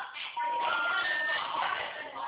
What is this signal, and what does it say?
Special information tone, with noise.